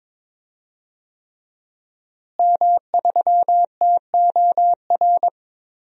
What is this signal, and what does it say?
Morse code, heard as a keyed tone, about 21 wpm, 700 Hz. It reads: M3TOR
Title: DXCC-100 Call Signs - 5 Characters - Repeated 3x - 22wpm